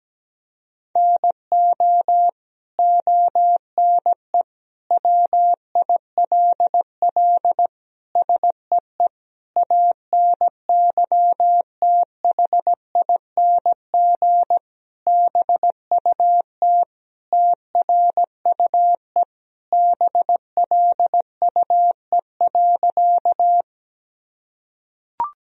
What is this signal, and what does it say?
Morse code, heard as a keyed tone, about 17 wpm, 700 Hz. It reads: NO ONE WILL SEE ANYTHING BUT TRUE BLUE.
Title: Sentences from Top 1000 Words - Review Part 3 - 17wpm